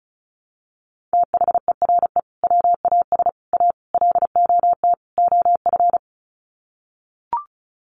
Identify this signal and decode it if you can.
Morse code, heard as a keyed tone, about 34 wpm, 700 Hz. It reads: THERE WAS A LOT OF
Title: Sets of 5 Words 35wpm